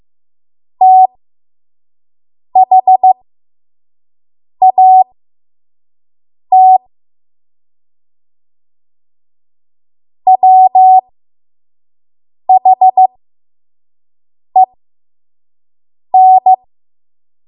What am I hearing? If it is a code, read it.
Morse code, heard as a keyed tone, about 15 wpm, spaced as for 5 wpm, 750 Hz. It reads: THAT WHEN
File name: Morse / 05WPM